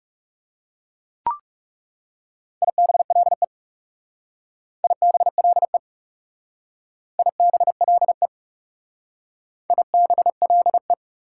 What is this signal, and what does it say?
Morse code, heard as a keyed tone, about 38 wpm, 700 Hz. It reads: IBLE IBLE IBLE IBLE